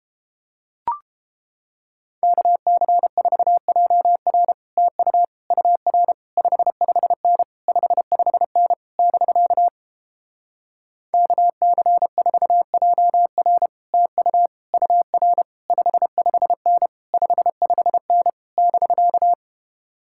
Morse code, heard as a keyed tone, about 29 wpm, 700 Hz. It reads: KC4JR TU UR 55N 55N <BK> KC4JR TU UR 55N 55N <BK>